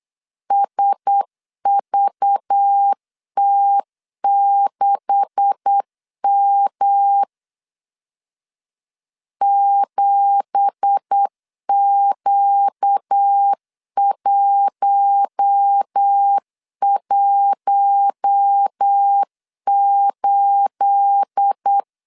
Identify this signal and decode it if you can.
Morse code, heard as a keyed tone, about 8 wpm, 800 Hz. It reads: SVT6M 7Q118
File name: CW / DET / 1 9 1500C S C 8wpm 5653